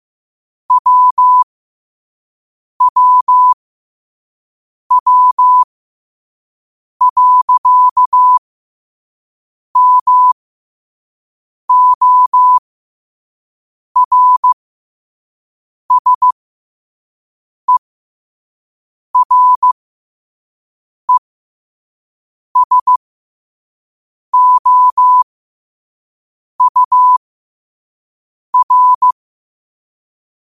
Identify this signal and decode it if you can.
Morse code, heard as a keyed tone, about 15 wpm, spaced as for 5 wpm, 1000 Hz. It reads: WWW.MORSERESOUR
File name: qotdfiles / 5wpm